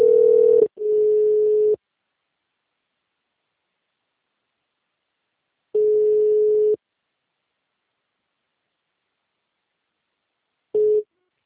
Ringback tone (Europe, 425 Hz).